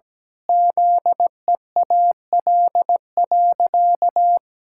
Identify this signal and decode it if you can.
Morse code, heard as a keyed tone, about 17 wpm, 700 Hz. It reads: ZEAL.
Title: Pangram Challenge v2 - 17wpm